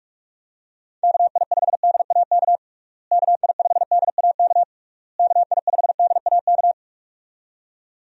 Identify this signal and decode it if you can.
Morse code, heard as a keyed tone, about 46 wpm, 700 Hz. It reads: KI5DAK KI5DAK KI5DAK
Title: USA Call Signs - 6 Characters - Repeated 3x - 45wpm